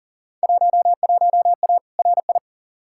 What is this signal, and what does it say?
Morse code, heard as a keyed tone, about 40 wpm, 700 Hz. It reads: 11A RI
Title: Field Day Exchanges Encore - 40wpm